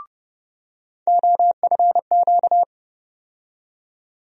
Morse code, heard as a keyed tone, about 32 wpm, 700 Hz. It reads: OFQ
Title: USA Call Sign Postfixes - 3 Letters 30wpm